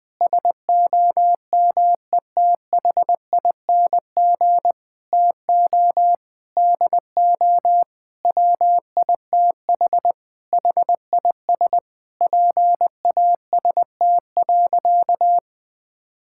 Morse code, heard as a keyed tone, about 20 wpm, 700 Hz. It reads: SOMETHING TO DO WITH HIS PAST.
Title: Sentences with 5-Word Phrases - Spoken Beginning - 20wpm